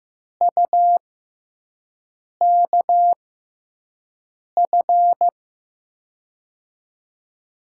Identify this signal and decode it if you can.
Morse code, heard as a keyed tone, about 16 wpm, spaced as for 5 wpm, 700 Hz. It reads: UKF
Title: USA Call Signs - 6 Characters - 6x Character Spacing - 15wpm